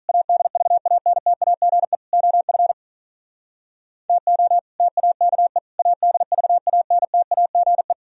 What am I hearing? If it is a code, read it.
Morse code, heard as a keyed tone, about 46 wpm, 700 Hz. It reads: ADVANTAGE OF TO TAKE ADVANTAGE